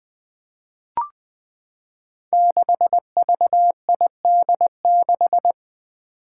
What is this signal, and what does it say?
Morse code, heard as a keyed tone, about 20 wpm, 700 Hz. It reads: 6VID6
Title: Five Challenging Letter-Number Combinations 20wpm